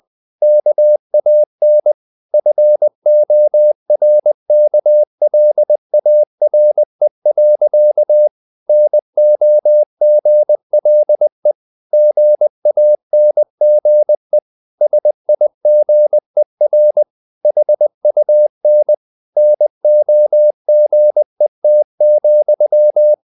Morse code, heard as a keyed tone, about 20 wpm, 600 Hz. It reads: KAN FORKLARE. NOGLE GANGE SIGER HUN NOGET,
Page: Ferie 20 wpm